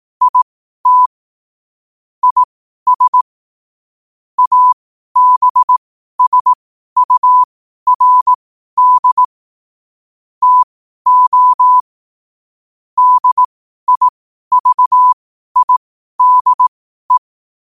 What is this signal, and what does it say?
Morse code, heard as a keyed tone, about 18 wpm, spaced as for 12 wpm, 1000 Hz. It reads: IT IS ABSURD TO DIVIDE
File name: qotdfiles / 12wpm